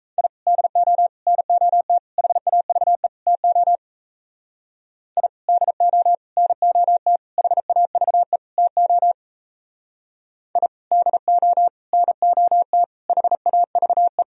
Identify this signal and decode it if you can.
Morse code, heard as a keyed tone, about 35 wpm, 700 Hz. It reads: I DO NOT HAVE TO I DO NOT HAVE TO I DO NOT HAVE